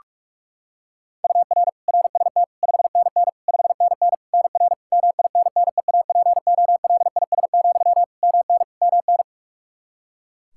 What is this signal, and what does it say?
Morse code, heard as a keyed tone, about 46 wpm, 700 Hz. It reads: UR RST 5NN 5NN NR MINNEAPOLIS, MN MN
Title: Practice QSO - Exchange 1 - Beginning - 45wpm